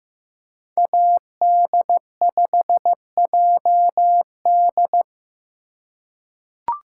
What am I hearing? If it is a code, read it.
Morse code, heard as a keyed tone, about 15 wpm, 700 Hz. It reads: AD5JD
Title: USA Call Signs - 5 Characters - 4x Character Spacing - 15wpm